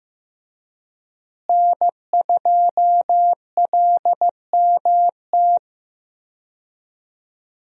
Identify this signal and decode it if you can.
Morse code, heard as a keyed tone, about 15 wpm, 700 Hz. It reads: N2LMT
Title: USA Call Signs - 5 Characters - Repeated 3x - 15wpm